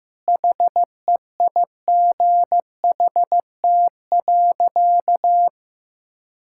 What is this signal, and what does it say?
Morse code, heard as a keyed tone, about 15 wpm, 700 Hz. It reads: HEIGHT.
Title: Sentences from Top 1600 Words 15wpm